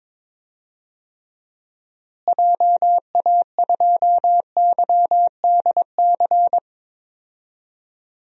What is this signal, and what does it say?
Morse code, heard as a keyed tone, about 22 wpm, 700 Hz. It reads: JA2YDC
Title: DXCC-100 Call Signs - 6 Characters - Repeated 3x - 22wpm